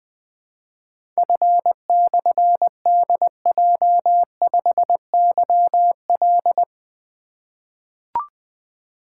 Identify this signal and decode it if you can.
Morse code, heard as a keyed tone, about 20 wpm, 700 Hz. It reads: F/DJ5YL
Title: Difficult Call Signs 20wpm